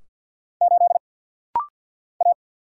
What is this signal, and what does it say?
Morse code, heard as a keyed tone, about 50 wpm, 700 Hz. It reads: 8 A
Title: Single Letter-Number Alt - Rapid-Fire 50wpm